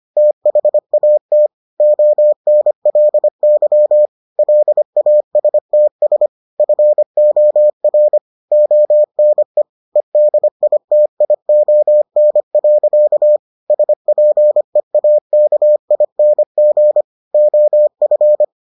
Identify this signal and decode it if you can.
Morse code, heard as a keyed tone, about 25 wpm, 600 Hz. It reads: THAT ONLY LASTS FOR ONE EDITION. SPEAKING OF